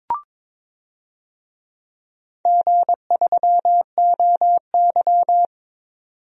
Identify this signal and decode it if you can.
Morse code, heard as a keyed tone, about 22 wpm, 700 Hz. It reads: G3OY
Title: DXCC-100 Call Signs - 4 Characters - 22wpm